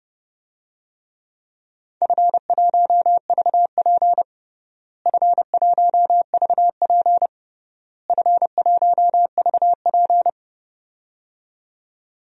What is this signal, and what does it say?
Morse code, heard as a keyed tone, about 30 wpm, 700 Hz. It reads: F1VP F1VP F1VP
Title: DXCC-100 Call Signs - 4 Characters - Repeated 3x - 30wpm